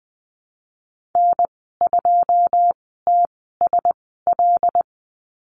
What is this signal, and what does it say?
Morse code, heard as a keyed tone, about 19 wpm, spaced as for 14 wpm, 700 Hz. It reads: N2TSL
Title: 088 Lesson 15g - Number 5 - Call Signs 20wpm